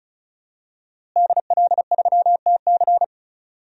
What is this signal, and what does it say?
Morse code, heard as a keyed tone, about 34 wpm, 700 Hz. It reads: DL3TC
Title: DXCC-100 Call Signs - 5 Characters - Repeated 3x - 35wpm